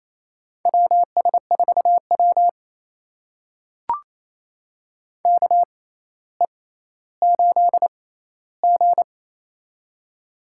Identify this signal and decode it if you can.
Morse code, heard as a keyed tone, about 28 wpm, spaced as for 14 wpm, 700 Hz. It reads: WS4W KE8G
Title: USA Call Signs - 4 Characters - 6x Character Spacing - 28wpm